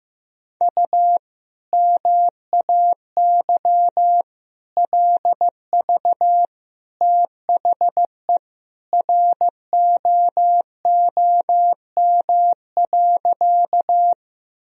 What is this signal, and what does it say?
Morse code, heard as a keyed tone, about 15 wpm, 700 Hz. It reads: U MAY LV THE ROOM.
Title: Sentences from Top 400 Words and Abbreviations 15wpm